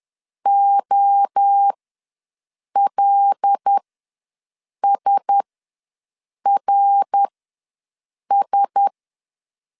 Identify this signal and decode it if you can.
Morse code, heard as a keyed tone, about 10 wpm, spaced as for 6 wpm, 800 Hz. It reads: OLSRS
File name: LEZIONI / 5 LBVC / 5 L ws 10wpm 3636